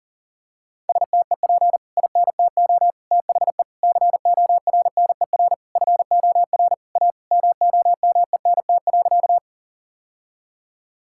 Morse code, heard as a keyed tone, about 40 wpm, 700 Hz. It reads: STEP INTO THE CORNER FOR A MOMENT.
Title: Sentences from Top 1600 Words 40wpm